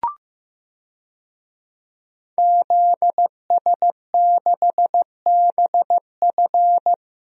Morse code, heard as a keyed tone, about 16 wpm, 700 Hz. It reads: ZS6BF
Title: DXCC-100 Call Signs - 5 Characters - Repeated 2x - 15wpm